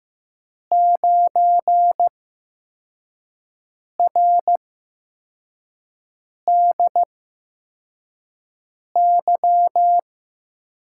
Morse code, heard as a keyed tone, about 15 wpm, spaced as for 4 wpm, 700 Hz. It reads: 9RDY
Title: USA Call Signs - 8x Character Spacing - 15wpm